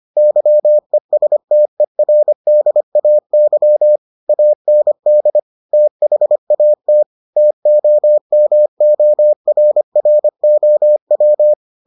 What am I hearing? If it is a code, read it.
Morse code, heard as a keyed tone, about 25 wpm, 600 Hz. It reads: YESTERDAY AND THAT TOMORROW